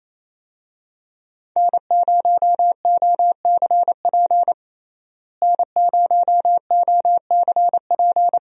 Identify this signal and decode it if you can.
Morse code, heard as a keyed tone, about 28 wpm, 700 Hz. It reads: N0OCP N0OCP
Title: USA Call Signs - 5 Characters - Repeated 3x - 28wpm